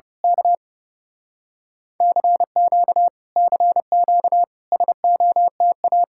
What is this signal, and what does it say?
Morse code, heard as a keyed tone, about 28 wpm, 700 Hz. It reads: K CQ CQ SOTA
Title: Speed-Racing Common QSO Phrases - 30wpm